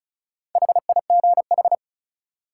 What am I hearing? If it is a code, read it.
Morse code, heard as a keyed tone, about 36 wpm, 700 Hz. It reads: HIGH
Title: Sentences from Top 200 Words - New Word Drill - 35wpm